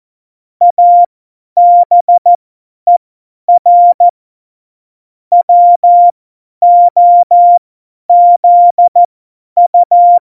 Morse code, heard as a keyed tone, about 14 wpm, spaced as for 10 wpm, 700 Hz. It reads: ABER WOZU
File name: Morsekurs / Witze / Witz 08. September 2020, 50 F70